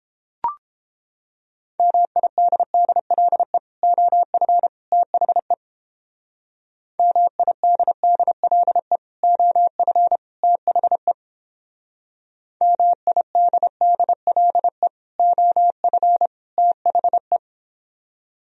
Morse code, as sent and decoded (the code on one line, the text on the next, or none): -- .. -.. -.. .-.. / --- ..-. / - .... / -- .. -.. -.. .-.. / --- ..-. / - .... / -- .. -.. -.. .-.. / --- ..-. / - .... .
MIDDLE OF THE MIDDLE OF THE MIDDLE OF THE